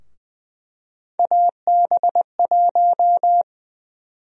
Morse code, heard as a keyed tone, about 20 wpm, 700 Hz. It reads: AB1
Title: USA Call Sign Prefixes - 20wpm